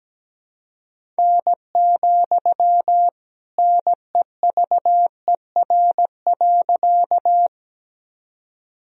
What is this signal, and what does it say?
Morse code, heard as a keyed tone, about 17 wpm, 700 Hz. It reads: N, NEVER.
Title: Sentences from Top 300 Words and Abbreviations 17wpm